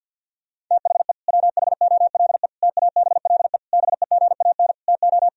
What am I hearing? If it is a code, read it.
Morse code, heard as a keyed tone, about 48 wpm, 700 Hz. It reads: THE WHOLE TABLE BEGAN TO
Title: Sentences from Top 1000 Words - Review Part 2 - 50wpm